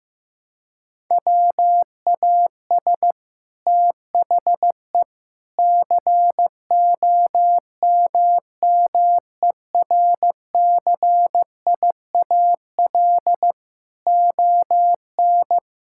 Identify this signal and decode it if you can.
Morse code, heard as a keyed tone, about 15 wpm, 700 Hz. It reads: WAS THE COMMERCIAL ON
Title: Sentences from Top 1000 Words - Review Part 5 - 15wpm